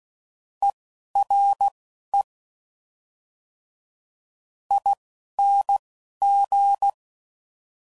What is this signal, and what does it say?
Morse code, heard as a keyed tone, about 16 wpm, spaced as for 11 wpm, 800 Hz. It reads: ERE ING
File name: Suffixes / General / ERE INGs